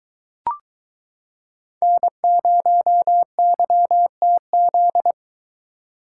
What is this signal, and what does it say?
Morse code, heard as a keyed tone, about 22 wpm, 700 Hz. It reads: N0YTZ